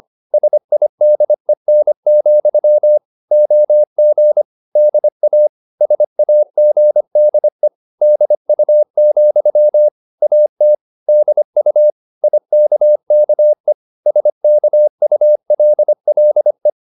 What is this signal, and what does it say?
Morse code, heard as a keyed tone, about 25 wpm, 600 Hz. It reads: SIDEN, OG DA SAGDE DU, AT DU IKKE SKULLE
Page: Ferie 25 wpm